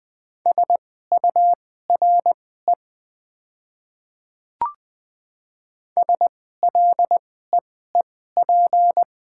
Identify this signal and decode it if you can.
Morse code, heard as a keyed tone, about 20 wpm, spaced as for 14 wpm, 700 Hz. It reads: SURE SLEEP
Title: 2x Character Spacing - Top 500 Words - 20wpm